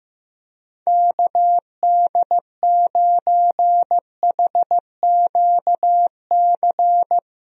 Morse code, heard as a keyed tone, about 15 wpm, 700 Hz. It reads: KD9HQC
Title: USA Call Signs - 6 Characters - Repeated 3x - 15wpm